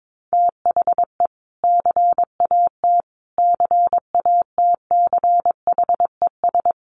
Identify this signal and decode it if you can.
Morse code, heard as a keyed tone, about 22 wpm, 700 Hz. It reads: THE CAT CATCHES
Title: Additional Easy Phrases 22wpm